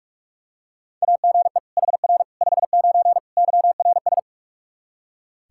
Morse code, heard as a keyed tone, about 46 wpm, 700 Hz. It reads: AGE HR 59 YRS
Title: K1USN SST QSO Practice 45wpm